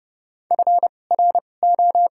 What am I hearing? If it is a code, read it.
Morse code, heard as a keyed tone, about 30 wpm, spaced as for 22 wpm, 700 Hz. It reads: FRO
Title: 2x Character Spacing - Top 200 Words - 30wpm